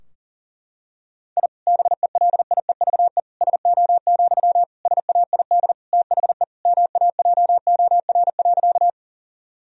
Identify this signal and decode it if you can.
Morse code, heard as a keyed tone, about 40 wpm, 700 Hz. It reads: I BELIEVE SO, SAID THE MAJOR.
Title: Sentences from Top 1000 Words - Review Part 4 - 40wpm